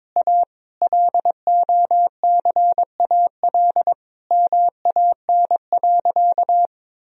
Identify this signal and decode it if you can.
Morse code, heard as a keyed tone, about 22 wpm, 700 Hz. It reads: A LOCAL MAN.
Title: Sentences from Top 1000 Words - Review Part 3 - 22wpm